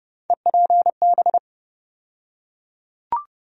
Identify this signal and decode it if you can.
Morse code, heard as a keyed tone, about 30 wpm, 700 Hz. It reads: EPB
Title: USA Call Sign Postfixes - 3 Letters 30wpm